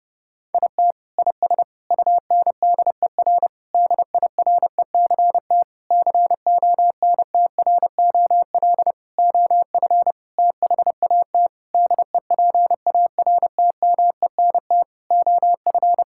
Morse code, heard as a keyed tone, about 30 wpm, 700 Hz. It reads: IT IS UNDER DIRECT CONTROL OF THAT DEPARTMENT OF
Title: Sentences from Top 1000 Words - Review Part 5 - 30wpm